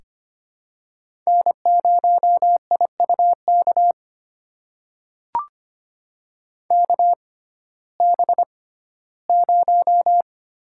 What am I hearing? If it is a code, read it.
Morse code, heard as a keyed tone, about 25 wpm, spaced as for 15 wpm, 700 Hz. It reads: N0IUK KB0
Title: USA Call Signs - 6x Character Spacing - 25wpm